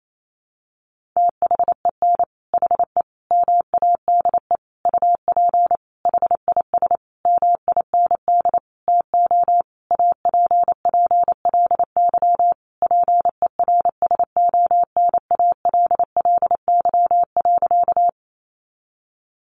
Morse code, heard as a keyed tone, about 28 wpm, 700 Hz. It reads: THEN HE MADE UP HIS MIND TO APPLY PERSONALLY.
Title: Sentences from Top 1400 Words 28wpm